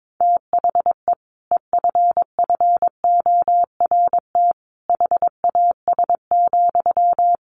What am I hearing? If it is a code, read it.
Morse code, heard as a keyed tone, about 22 wpm, 700 Hz. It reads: THE EFFORT HAS,
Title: Sentences from Top 1200 Words 22wpm